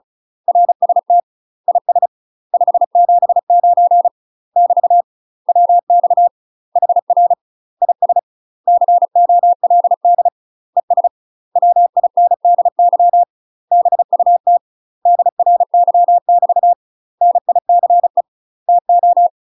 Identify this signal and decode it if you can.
Morse code, heard as a keyed tone, about 35 wpm, 700 Hz. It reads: RST IS 579 <BT> WX HR IS COLD ES WINDY BUT DRY <BT> NICE TO
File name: UK MASTER QSOs TRAINING / 35WPM / UK MASTER QSOs 0035